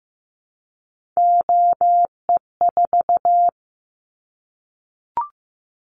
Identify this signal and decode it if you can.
Morse code, heard as a keyed tone, about 15 wpm, 700 Hz. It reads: OE4
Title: Callsign Prefixes for Top 50 DXCC - 15wpm